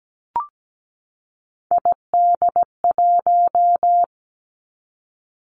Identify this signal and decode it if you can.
Morse code, heard as a keyed tone, about 17 wpm, 700 Hz. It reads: ID1
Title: Three Challenging Letter-Number Combinations 17wpm